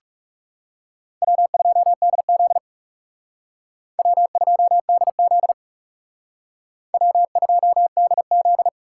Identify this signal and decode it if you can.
Morse code, heard as a keyed tone, about 39 wpm, 700 Hz. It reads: W2DZ W2DZ W2DZ